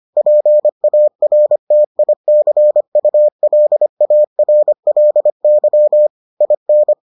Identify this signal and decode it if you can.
Morse code, heard as a keyed tone, about 25 wpm, 600 Hz. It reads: PARTICULARLY IN